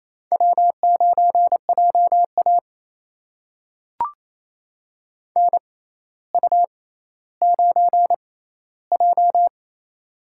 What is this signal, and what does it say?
Morse code, heard as a keyed tone, about 28 wpm, spaced as for 14 wpm, 700 Hz. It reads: W9JA NU9J
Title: USA Call Signs - 4 Characters - 6x Character Spacing - 28wpm